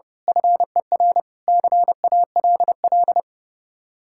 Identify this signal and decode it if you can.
Morse code, heard as a keyed tone, about 30 wpm, 700 Hz. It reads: FER CALL
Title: Speed-Racing Common QSO Phrases Encore - 30wpm